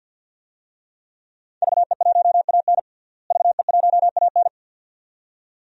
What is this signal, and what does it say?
Morse code, heard as a keyed tone, about 47 wpm, 700 Hz. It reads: VE1AN VE1AN
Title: DXCC-100 Call Signs - 5 Characters - Repeated 2x - 50wpm